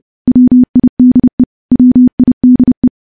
Morse code, heard as a keyed tone, about 30 wpm, 250 Hz. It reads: WIDE WIDE
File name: morse / morse-training-30WPM